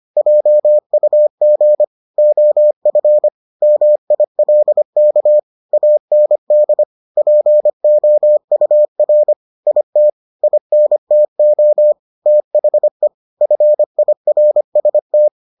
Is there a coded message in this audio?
Yes: Morse code from 0.2 to 15.3 s, heard as a keyed tone, about 25 wpm, 600 Hz. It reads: JUG OF MILK AND POUR IT INTO THE FIRST